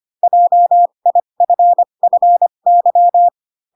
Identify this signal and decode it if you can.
Morse code, heard as a keyed tone, about 25 wpm, spaced as for 22 wpm, 700 Hz. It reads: JIFFY